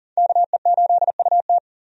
Morse code, heard as a keyed tone, about 37 wpm, 700 Hz. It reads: KE8UT
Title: USA Call Signs - 5 Characters - 40wpm